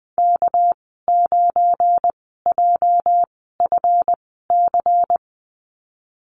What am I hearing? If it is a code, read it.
Morse code, heard as a keyed tone, about 20 wpm, spaced as for 15 wpm, 700 Hz. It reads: K9JFC
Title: 238 Lesson 38g - Letter q - Call Signs 20wpm